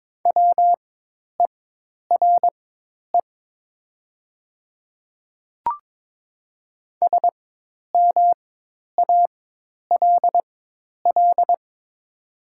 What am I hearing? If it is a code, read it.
Morse code, heard as a keyed tone, about 22 wpm, spaced as for 10 wpm, 700 Hz. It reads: WERE SMALL
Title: 4x Character Spacing - Top 200 Words - 22wpm